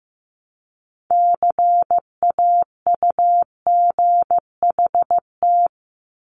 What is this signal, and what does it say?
Morse code, heard as a keyed tone, about 15 wpm, 700 Hz. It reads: CAUGHT
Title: Multi-letter Phonemes 15wpm